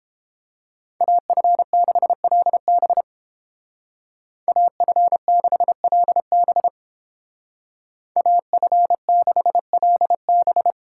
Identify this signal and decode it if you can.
Morse code, heard as a keyed tone, about 29 wpm, 700 Hz. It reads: AF6LB AF6LB AF6LB